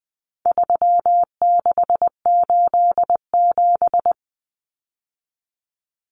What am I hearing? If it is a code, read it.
Morse code, heard as a keyed tone, about 20 wpm, 700 Hz. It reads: 3687
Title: Numbers - 4-Digit 20wpm